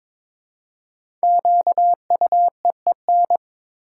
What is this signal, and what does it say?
Morse code, heard as a keyed tone, about 22 wpm, 690 Hz. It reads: QUEEN
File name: 5 Letter Words - First 3 Letters Given - 22wpm